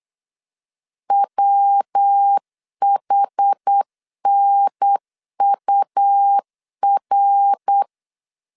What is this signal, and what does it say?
Morse code, heard as a keyed tone, about 8 wpm, 800 Hz. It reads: WHNUR
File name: CW / DET / 1 3 1500C S B 8wpm 4141